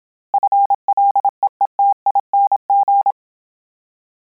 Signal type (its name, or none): Morse code